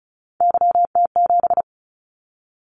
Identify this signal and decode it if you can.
Morse code, heard as a keyed tone, about 36 wpm, 700 Hz. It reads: YT7
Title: Callsign Prefixes for Top 25 DXCC - 35wpm